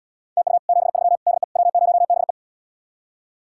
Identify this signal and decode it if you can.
Morse code, heard as a keyed tone, about 77 wpm, 700 Hz. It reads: TU 73 DE W0BE